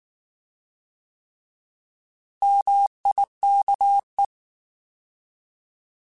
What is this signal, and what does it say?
Morse code, heard as a keyed tone, about 19 wpm, 800 Hz. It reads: MIKE
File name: Alphabetical / Alphabetical mp3s / Ms @18